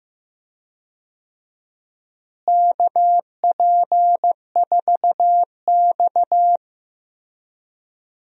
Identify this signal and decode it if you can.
Morse code, heard as a keyed tone, about 15 wpm, 700 Hz. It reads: KP4X
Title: DXCC-100 Call Signs - 4 Characters - Repeated 3x - 15wpm